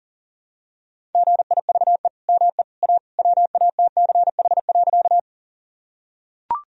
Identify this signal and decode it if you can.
Morse code, heard as a keyed tone, about 40 wpm, 700 Hz. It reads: GIVE ME A WATCH.
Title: Sentences from Top 1000 Words - Review Part 1 - 40wpm